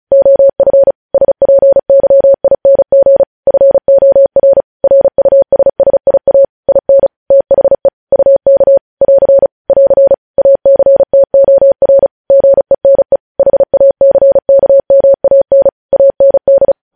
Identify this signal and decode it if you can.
Morse code, heard as a keyed tone, about 36 wpm, 550 Hz. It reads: OF SPYING FOR RUSSIA IN THE UK <AR> <AR> ACTOR GENE HACKMAN AND
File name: podcast / News headlines-1x-35x35